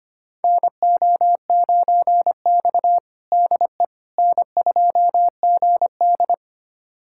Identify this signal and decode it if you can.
Morse code, heard as a keyed tone, about 25 wpm, 700 Hz. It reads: NO9X DE N2GD
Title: USA Callsign DE Callsign - 4 Characters - 25wpm